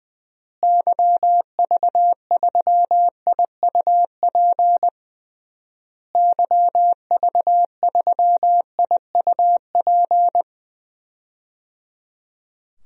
Morse code, heard as a keyed tone, about 20 wpm, 700 Hz. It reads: YV3IUP YV3IUP
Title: DXCC-100 Call Signs - 6 Characters - Repeated 3x - 20wpm